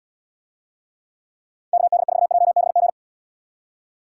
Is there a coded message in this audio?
Yes: Morse code from 1.7 to 2.9 s, heard as a keyed tone, about 75 wpm, 700 Hz. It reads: KD4JBG